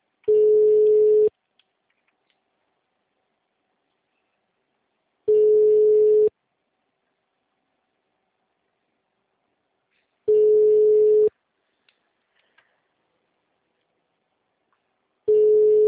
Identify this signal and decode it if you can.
Ringback tone (Europe, 425 Hz), with speech.